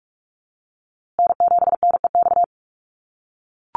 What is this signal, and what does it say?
Morse code, heard as a keyed tone, about 44 wpm, 700 Hz. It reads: N7NEX
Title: USA Call Signs Repeated 2x 45wpm